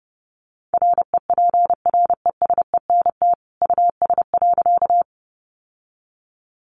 Morse code, heard as a keyed tone, about 30 wpm, 700 Hz. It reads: REPRESENT US.
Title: Sentences from Top 1200 Words 30wpm